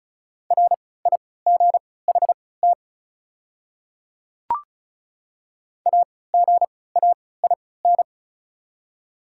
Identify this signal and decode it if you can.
Morse code, heard as a keyed tone, about 36 wpm, spaced as for 20 wpm, 700 Hz. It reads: RIGHT AGAIN
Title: 3x Character Spacing - Top 200 Words - 35wpm